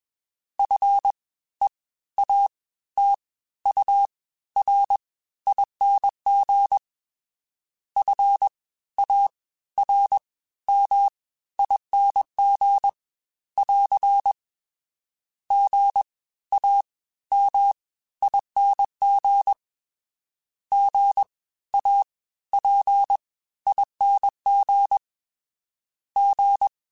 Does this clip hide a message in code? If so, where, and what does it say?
Yes: Morse code from 0.6 to 26.7 s, heard as a keyed tone, about 21 wpm, spaced as for 14 wpm, 800 Hz. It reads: FEATURING FARMING <AR> GAMING GAPING G